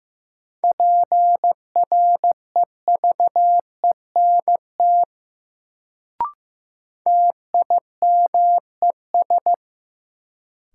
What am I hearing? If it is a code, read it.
Morse code, heard as a keyed tone, about 15 wpm, 700 Hz. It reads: PREVENT TIMES
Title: Sentences from Top 1100 Words - New Word Drill Alt - 15wpm